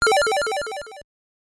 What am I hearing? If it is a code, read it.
There is no code.